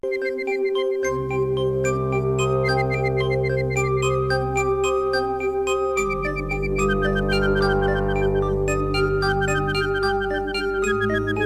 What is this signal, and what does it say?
There is no code.